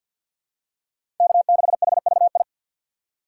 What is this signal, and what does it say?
Morse code, heard as a keyed tone, about 46 wpm, 690 Hz. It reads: K6HVI
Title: USA Call Signs - 5 Characters - 6x Character Spacing - 50wpm